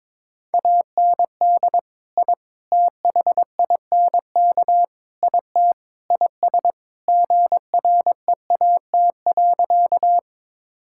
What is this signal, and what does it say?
Morse code, heard as a keyed tone, about 22 wpm, 700 Hz. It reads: AND I THINK IT IS GREAT.
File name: practice / 22wpm / Sentences with 4-Word Phrases - 22wpm